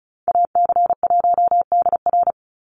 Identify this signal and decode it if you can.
Morse code, heard as a keyed tone, about 34 wpm, 700 Hz. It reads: AC1DR
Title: USA Call Signs 35wpm